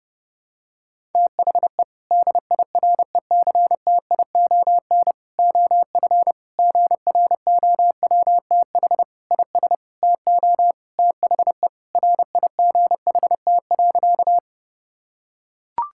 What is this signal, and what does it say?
Morse code, heard as a keyed tone, about 30 wpm, 700 Hz. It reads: THE DIRECTION OF GROWTH IS TO THE RIGHT.
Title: Sentences from Top 1000 Words - Review Part 5 - 30wpm